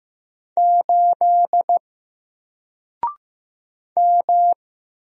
Morse code, heard as a keyed tone, about 15 wpm, 700 Hz. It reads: 8 M
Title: Single Letter-Number 15wpm